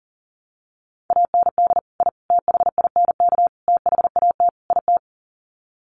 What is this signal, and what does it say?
Morse code, heard as a keyed tone, about 40 wpm, 700 Hz. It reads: AND I THINK THAT IT
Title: Sets of 5 Words Encore2 40wpm